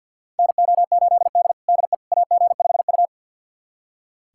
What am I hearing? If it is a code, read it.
Morse code, heard as a keyed tone, about 45 wpm, 700 Hz. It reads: NO8D DE AG5U
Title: USA Callsign DE Callsign - 4 Characters - 50wpm